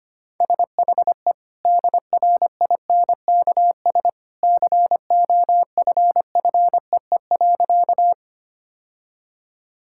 Morse code, heard as a keyed tone, about 25 wpm, 690 Hz. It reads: SHE DRINKS COFFEE.
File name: Easy Sentences with Hints - 25wpm